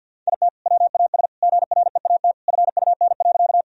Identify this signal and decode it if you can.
Morse code, heard as a keyed tone, about 50 wpm, 700 Hz. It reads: IT WAS GREAT FUN.
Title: Sentences from Top 1000 Words - Review Part 4 - 50wpm